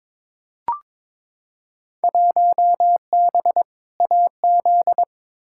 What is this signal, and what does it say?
Morse code, heard as a keyed tone, about 22 wpm, 700 Hz. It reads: 1B AZ
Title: Field Day Exchanges Encore - 22wpm